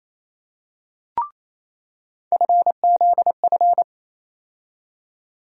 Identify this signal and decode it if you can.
Morse code, heard as a keyed tone, about 28 wpm, 700 Hz. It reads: FZF
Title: USA Call Sign Postfixes - 3 Letters 28wpm